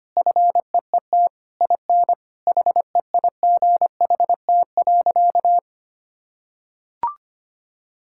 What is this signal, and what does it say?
Morse code, heard as a keyed tone, about 25 wpm, 700 Hz. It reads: FEET IN HEIGHT.
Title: Sentences from Top 1600 Words 25wpm